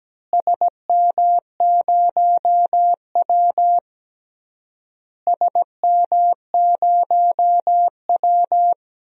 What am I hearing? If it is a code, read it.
Morse code, heard as a keyed tone, about 17 wpm, 700 Hz. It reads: SM0W SM0W
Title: DXCC-100 Call Signs - 4 Characters - Repeated 3x - 17wpm